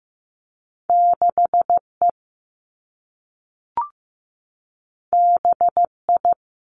Morse code, heard as a keyed tone, about 15 wpm, 700 Hz. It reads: 6 E B I
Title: Two Challenging Letter-Number Combinations 15wpm